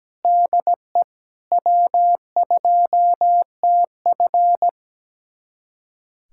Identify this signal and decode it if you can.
Morse code, heard as a keyed tone, about 17 wpm, 700 Hz. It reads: DE W2TF
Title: USA Callsign DE Callsign - 4 Characters - 17wpm